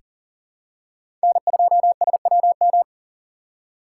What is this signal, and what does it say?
Morse code, heard as a keyed tone, about 40 wpm, 700 Hz. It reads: N2SWM
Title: USA Call Signs - 5 Characters - 6x Character Spacing - 40wpm